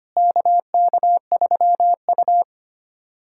Morse code, heard as a keyed tone, about 25 wpm, 700 Hz. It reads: KK3U
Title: USA Callsign DE Callsign - 4 Characters - 25wpm